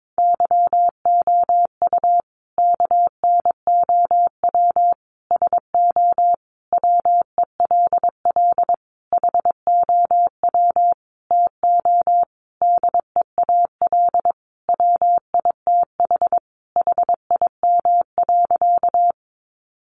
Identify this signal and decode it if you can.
Morse code, heard as a keyed tone, about 22 wpm, 700 Hz. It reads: YOU KNOW SO WELL HOW TO DEAL WITH HIM.
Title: Sentences from Top 800 Words 22wpm